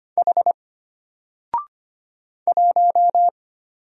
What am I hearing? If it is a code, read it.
Morse code, heard as a keyed tone, about 25 wpm, 700 Hz. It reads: H 1
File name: Single-Letter-Number-25wpm